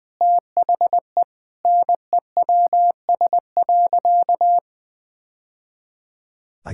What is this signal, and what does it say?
Morse code, heard as a keyed tone, about 21 wpm, 700 Hz. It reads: THE NEWS.
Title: Sentences with 2-Word Phrases - Spoken Beginning - 20wpm